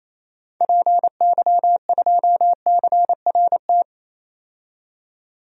Morse code, heard as a keyed tone, about 28 wpm, 700 Hz. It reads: PY2CRT
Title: DXCC-100 Call Signs - 6 Characters - 28wpm